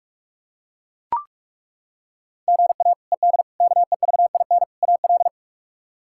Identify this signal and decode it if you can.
Morse code, heard as a keyed tone, about 43 wpm, 700 Hz. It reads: GA ED KEVIN AL
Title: K1USN SST QSO Practice - 45wpm